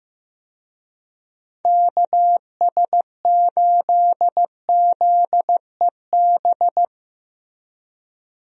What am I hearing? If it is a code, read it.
Morse code, heard as a keyed tone, about 15 wpm, 700 Hz. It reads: KS8ZEB
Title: USA Call Signs - 6 Characters - Repeated 3x - 15wpm